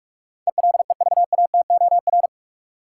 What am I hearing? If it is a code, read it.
Morse code, heard as a keyed tone, about 42 wpm, 700 Hz. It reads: ELEVATOR
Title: Sentences from Top 1600 Words - New Word Drill - 45wpm